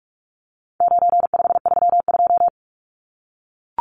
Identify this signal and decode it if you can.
Morse code, heard as a keyed tone, about 44 wpm, 700 Hz. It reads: 9532
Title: Numbers - 4-Digit 45wpm